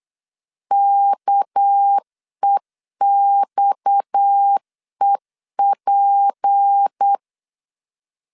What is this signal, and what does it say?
Morse code, heard as a keyed tone, about 8 wpm, 800 Hz. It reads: KEXEP
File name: CW / DET / 1 6 1500C S B 8wpm 5142